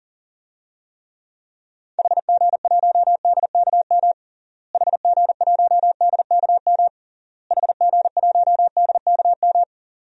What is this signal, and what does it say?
Morse code, heard as a keyed tone, about 40 wpm, 700 Hz. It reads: HG1DKM HG1DKM HG1DKM
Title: DXCC-100 Call Signs - 6 Characters - Repeated 3x - 40wpm